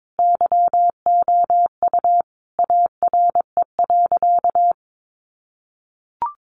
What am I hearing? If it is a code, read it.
Morse code, heard as a keyed tone, about 22 wpm, 700 Hz. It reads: YOU ARE.
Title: Sentences from Top 100 Words 22wpm